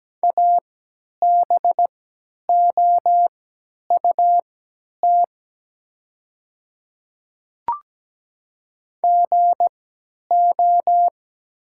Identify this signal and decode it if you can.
Morse code, heard as a keyed tone, about 17 wpm, spaced as for 10 wpm, 690 Hz. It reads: ABOUT GO
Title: 3x Character Spacing - Top 200 Words - 17wpm